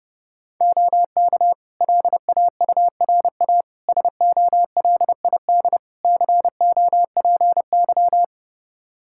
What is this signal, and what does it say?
Morse code, heard as a keyed tone, about 32 wpm, 700 Hz. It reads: OK LAURA SOLID COPY
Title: K1USN SST QSO Practice 30wpm